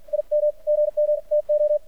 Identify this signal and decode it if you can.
Morse code, heard as a keyed tone, about 51 wpm, spaced as for 45 wpm, 600 Hz. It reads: IM GM TO
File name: archived / CRSnet